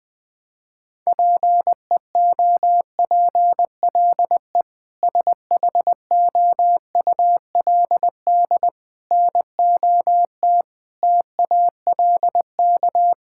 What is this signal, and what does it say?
Morse code, heard as a keyed tone, about 19 wpm, 700 Hz. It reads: PEOPLE SHOULD NOT TALK
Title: Sentences from Top 1000 Words - Review Part 6 - 20wpm